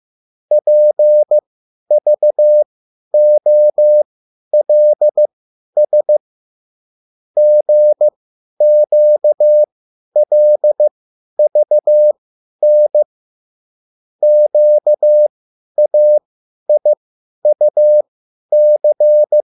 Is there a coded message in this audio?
Yes: Morse code from 0.5 to 19.4 s, heard as a keyed tone, about 15 wpm, spaced as for 10 wpm, 600 Hz. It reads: PVOLS GQLVN QAIUC